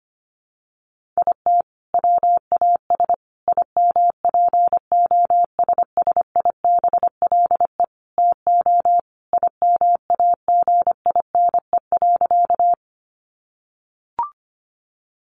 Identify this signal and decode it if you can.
Morse code, heard as a keyed tone, about 25 wpm, 700 Hz. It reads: IT WAS IMPOSSIBLE TO IMAGINE.
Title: Sentences from Top 1400 Words 25wpm